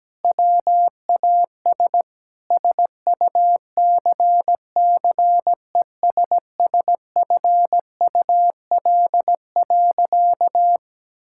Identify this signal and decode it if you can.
Morse code, heard as a keyed tone, about 17 wpm, 700 Hz. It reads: WAS SUCCESSFUL.
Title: Sentences from Top 1000 Words - Review Part 5 - 17wpm